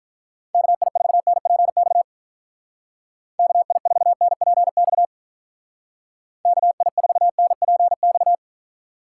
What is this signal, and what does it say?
Morse code, heard as a keyed tone, about 44 wpm, 700 Hz. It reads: KI4NPX KI4NPX KI4NPX
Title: Speed Racing - USA 2x3 Call Signs 35wpm